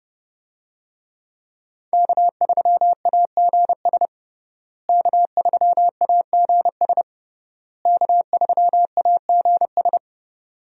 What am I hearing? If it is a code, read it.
Morse code, heard as a keyed tone, about 30 wpm, 700 Hz. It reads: K3AGS K3AGS K3AGS
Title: USA Call Signs - 5 Characters - Repeated 3x - 30wpm